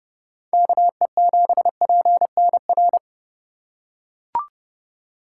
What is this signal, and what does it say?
Morse code, heard as a keyed tone, about 31 wpm, 700 Hz. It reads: KE7PNR